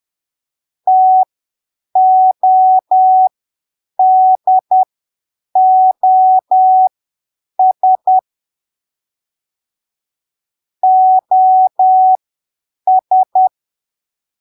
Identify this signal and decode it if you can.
Morse code, heard as a keyed tone, about 10 wpm, spaced as for 7 wpm, 750 Hz. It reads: TODOS OS
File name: arquivos / texto / texto 25 10ppm